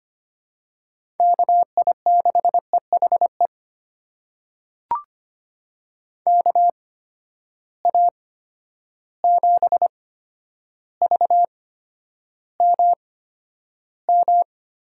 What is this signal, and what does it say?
Morse code, heard as a keyed tone, about 25 wpm, spaced as for 11 wpm, 700 Hz. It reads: KI6EHE KA7VMM
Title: USA Call Signs - 6 Characters - 8x Character Spacing - 25wpm